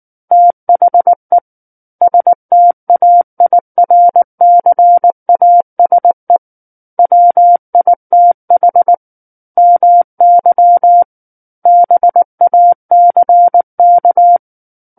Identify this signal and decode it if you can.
Morse code, heard as a keyed tone, about 19 wpm, 700 Hz. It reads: THE STAIRCASE WITH MY BACK
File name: Book One / 10-Chapter 10 - 19 WPM